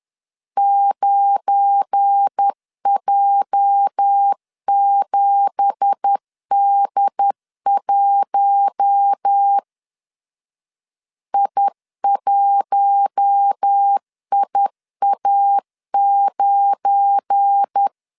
Morse code, heard as a keyed tone, about 10 wpm, 800 Hz. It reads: 9J7D1 I1IA9
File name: CW / DET / 1 9 1500C S B 10wpm 4439